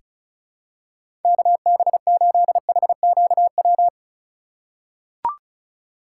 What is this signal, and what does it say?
Morse code, heard as a keyed tone, about 34 wpm, 700 Hz. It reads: KB8HQW
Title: USA Call Signs - 6 Characters - 4x Character Spacing - 35wpm